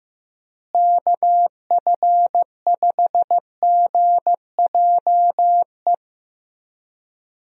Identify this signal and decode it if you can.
Morse code, heard as a keyed tone, about 15 wpm, 700 Hz. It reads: KF5GJE
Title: USA Call Signs - 6 Characters - Repeated 3x - 15wpm